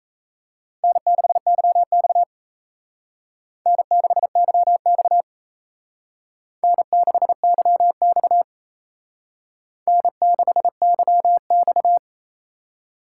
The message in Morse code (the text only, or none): N6YX N6YX N6YX N6YX